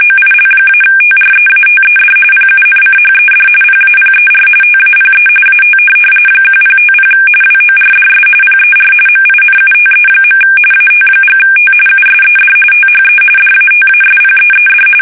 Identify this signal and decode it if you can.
RTTY, about 45.45 baud. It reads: OIL-TREATED PUNCHED PAPER TAPE AT 1050 WORDS PER MINUTE. AT&T USED THE DATASPEED
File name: fsk850 / teletype-corp